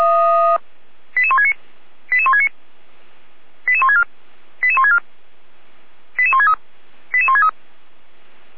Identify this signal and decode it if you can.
Five-tone selective call (ZVEI), heard as quick runs of five tones, tones of about 70 ms. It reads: 88169 88169 88164 88164 88163 88163